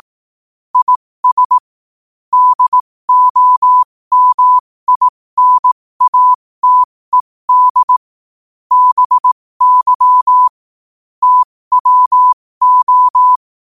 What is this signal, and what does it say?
Morse code, heard as a keyed tone, about 18 wpm, spaced as for 15 wpm, 1000 Hz. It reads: IS DOMINATED BY TWO
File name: qotdfiles / 15wpm